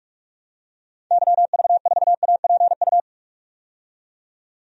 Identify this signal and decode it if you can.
Morse code, heard as a keyed tone, about 43 wpm, 700 Hz. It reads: YV4APU
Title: DXCC-100 Call Signs - 6 Characters - 45wpm